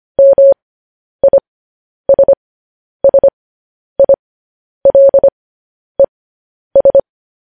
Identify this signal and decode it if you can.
Morse code, heard as a keyed tone, about 25 wpm, spaced as for 10 wpm, 550 Hz. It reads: MISSILES